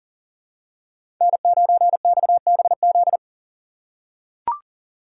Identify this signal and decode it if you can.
Morse code, heard as a keyed tone, about 40 wpm, 700 Hz. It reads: N9XBZ
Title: USA Call Signs - 5 Characters - 2x Character Spacing - 40wpm